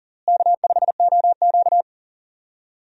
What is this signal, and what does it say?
Morse code, heard as a keyed tone, about 40 wpm, 700 Hz. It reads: K5OQ
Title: USA Call Signs - 4 Characters - Repeated 2x - 40wpm